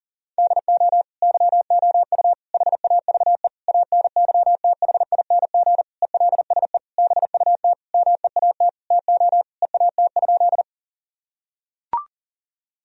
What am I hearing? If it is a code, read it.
Morse code, heard as a keyed tone, about 40 wpm, 700 Hz. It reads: DO YOU HAVE ANYTHING ELSE BUT MEAT TO EAT?
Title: Sentences from Top 1000 Words - Review Part 3 - 40wpm